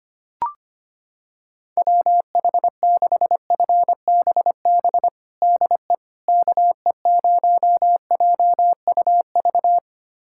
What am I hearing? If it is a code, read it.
Morse code, heard as a keyed tone, about 25 wpm, 700 Hz. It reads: WH6FBB DE KE0JUV
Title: USA Callsign DE Callsign 25wpm